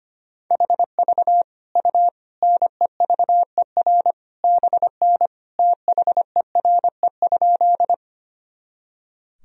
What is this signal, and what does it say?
Morse code, heard as a keyed tone, about 25 wpm, 700 Hz. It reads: HV U NEVER BN THERE?
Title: Sentences from Top 300 Words and Abbreviations 25wpm